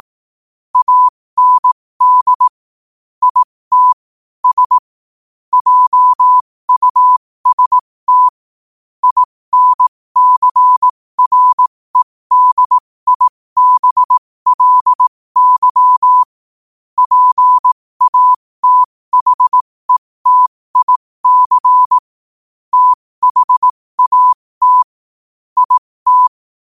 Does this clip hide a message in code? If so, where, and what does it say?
Yes: Morse code from 0.8 to 26.3 s, heard as a keyed tone, about 18 wpm, spaced as for 15 wpm, 1000 Hz. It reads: AND IT S JUST INCREDIBLY PATHETIC THAT IT